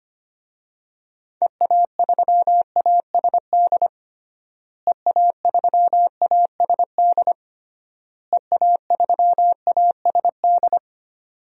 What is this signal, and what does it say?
Morse code, heard as a keyed tone, about 25 wpm, 700 Hz. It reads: EA3ASD EA3ASD EA3ASD
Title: DXCC-100 Call Signs - 6 Characters - Repeated 3x - 25wpm